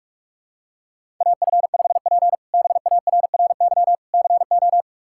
Morse code, heard as a keyed tone, about 47 wpm, 700 Hz. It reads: AF5P BARRY CO
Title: K1USN SST QSO Practice - 45wpm